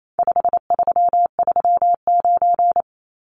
Morse code, heard as a keyed tone, about 28 wpm, 700 Hz. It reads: 5339
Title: Numbers - 4-Digit 28wpm